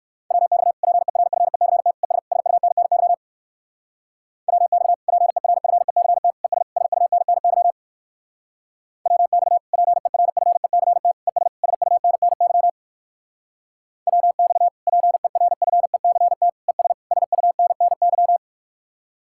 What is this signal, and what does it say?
Morse code, heard as a keyed tone, about 56 wpm, 700 Hz. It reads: WX PERFECT ES SUNNY WX PERFECT ES SUNNY WX PERFECT ES SUNNY WX PERFECT ES SUNNY
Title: Speed-Racing Common QSO Phrases - 45wpm